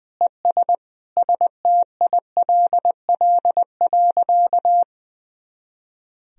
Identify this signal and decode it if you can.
Morse code, heard as a keyed tone, about 20 wpm, 700 Hz. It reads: ES STILL.
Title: Sentences from Top 500 Words and Abbreviations 20wpm